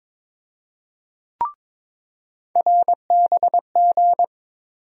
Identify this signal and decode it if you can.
Morse code, heard as a keyed tone, about 22 wpm, 700 Hz. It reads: RBG
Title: USA Call Sign Postfixes - 3 Letters 22wpm